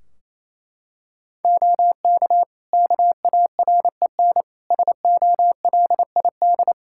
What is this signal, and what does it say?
Morse code, heard as a keyed tone, about 28 wpm, 700 Hz. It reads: OK KAREN SOLID
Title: Practice QSO - Exchange 3 - More Info - 28wpm